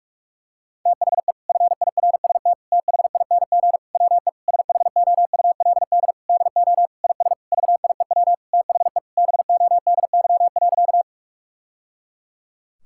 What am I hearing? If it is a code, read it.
Morse code, heard as a keyed tone, about 43 wpm, 700 Hz. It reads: THE FIRST THING WE SHOULD DO IS VIEW THE BODY.
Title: Sentences from Top 1000 Words - Review Part 6 - 45wpm